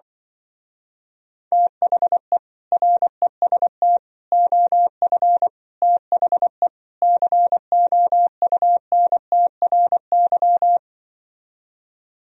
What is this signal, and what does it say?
Morse code, heard as a keyed tone, about 24 wpm, 700 Hz. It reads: THE REST OF THE COUNTRY
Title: Speed-Racing - Sets of 5 Words 20wpm